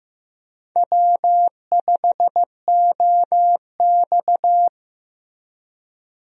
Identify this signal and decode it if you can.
Morse code, heard as a keyed tone, about 15 wpm, 700 Hz. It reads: W5OX
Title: USA Call Signs - 4 Characters - Repeated 2x - 15wpm